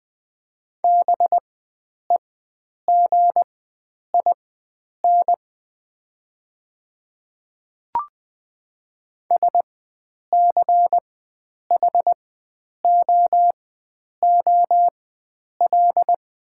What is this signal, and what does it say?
Morse code, heard as a keyed tone, about 20 wpm, spaced as for 9 wpm, 700 Hz. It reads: BEGIN SCHOOL
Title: 4x Character Spacing - Top 300 Words - 20wpm